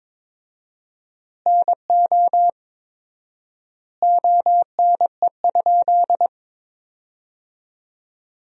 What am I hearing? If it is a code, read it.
Morse code, heard as a keyed tone, about 23 wpm, 700 Hz. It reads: NO ONE?
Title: 4x Word-Spacing - Sentences from Top 300 Words - 22wpm